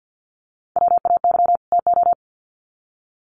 Morse code, heard as a keyed tone, about 50 wpm, 700 Hz. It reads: WAY TO
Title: Sets of 2 Words 50wpm